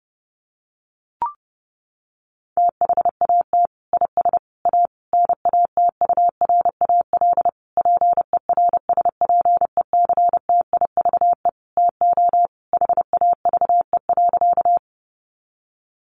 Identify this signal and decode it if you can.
Morse code, heard as a keyed tone, about 30 wpm, 700 Hz. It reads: THAT IS A NATURAL PERSPECTIVE TO HAVE.
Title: Sentences from Top 1200 Words 30wpm